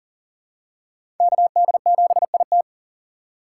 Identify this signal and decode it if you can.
Morse code, heard as a keyed tone, about 40 wpm, 700 Hz. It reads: KD7IT
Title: USA Call Signs - 2x Character Spacing - 40wpm